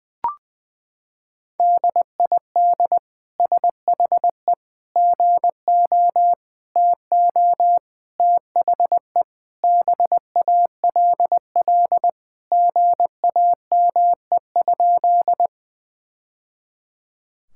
Morse code, heard as a keyed tone, about 20 wpm, 700 Hz. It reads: DID SHE GO TO THE BALL GAME?
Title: Sentences from Top 1000 Words - Review Part 6 - 20wpm